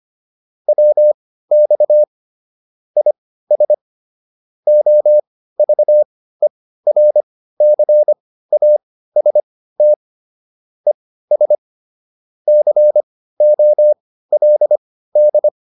Morse code, heard as a keyed tone, about 25 wpm, spaced as for 15 wpm, 600 Hz. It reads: WX IS OVERCAST ES COLD